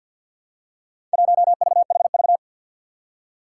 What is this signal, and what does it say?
Morse code, heard as a keyed tone, about 47 wpm, 700 Hz. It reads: 1VHV
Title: Four Challenging Letter-Number Combinations 50wpm